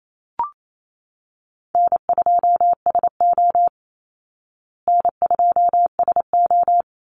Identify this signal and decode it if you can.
Morse code, heard as a keyed tone, about 28 wpm, 700 Hz. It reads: N2SO N2SO
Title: Easy Call Signs Repeated 3x - 28wpm